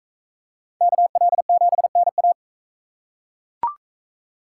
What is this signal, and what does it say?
Morse code, heard as a keyed tone, about 37 wpm, 700 Hz. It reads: KL7NA